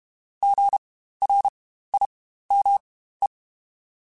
Morse code, heard as a keyed tone, about 32 wpm, spaced as for 15 wpm, 800 Hz. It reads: GRIME
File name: Spaced Alphas / Spaced Alphas mp3s / Gs sp @30